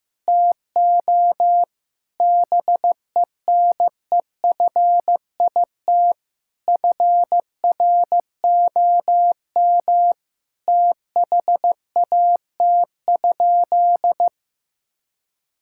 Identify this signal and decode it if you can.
Morse code, heard as a keyed tone, about 15 wpm, 700 Hz. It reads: TO BENEFIT FROM THAT?
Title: Sentences from Top 1000 Words - Review Part 6 - 15wpm